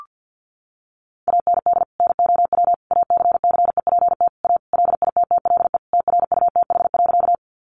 Morse code, heard as a keyed tone, about 50 wpm, 700 Hz. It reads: AND NOW ACCEPT A LITTLE TRUTH.
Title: Sentences from Top 1300 Words 50wpm